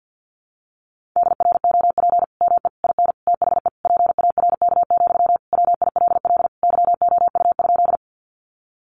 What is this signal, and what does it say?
Morse code, heard as a keyed tone, about 48 wpm, 700 Hz. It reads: DROP ME IN THE PARK, WILL YOU?
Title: Sentences from Top 1400 Words 50wpm